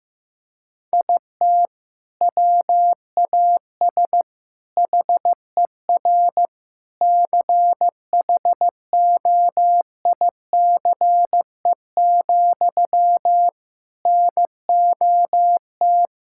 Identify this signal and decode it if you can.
Morse code, heard as a keyed tone, about 15 wpm, 700 Hz. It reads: IT WAS HER CHOICE, NOT
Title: Sentences from Top 1000 Words - Review Part 6 - 15wpm